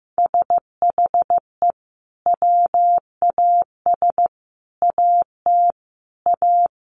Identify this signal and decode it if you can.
Morse code, heard as a keyed tone, about 15 wpm, 700 Hz. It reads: SHE WAS AT A
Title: Sentences from Top 1300 Words 15wpm